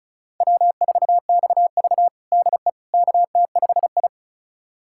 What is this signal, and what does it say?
Morse code, heard as a keyed tone, about 36 wpm, 700 Hz. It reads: W4XV DE KT5I
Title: USA Callsign DE Callsign - 4 Characters - 35wpm